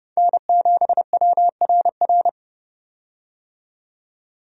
Morse code, heard as a keyed tone, about 29 wpm, 700 Hz. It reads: N7WRR
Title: USA Call Signs - 5 Characters - 30wpm